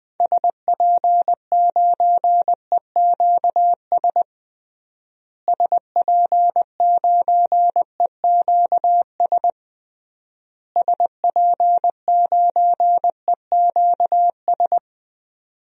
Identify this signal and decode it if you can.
Morse code, heard as a keyed tone, about 21 wpm, 700 Hz. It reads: SP9EQS SP9EQS SP9EQS
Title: DXCC-100 Call Signs - 6 Characters - Repeated 3x - 20wpm